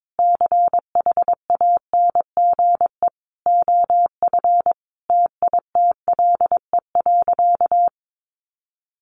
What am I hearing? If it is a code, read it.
Morse code, heard as a keyed tone, about 22 wpm, 700 Hz. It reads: CHANGE OF TITLE.
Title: Sentences from Top 1200 Words 22wpm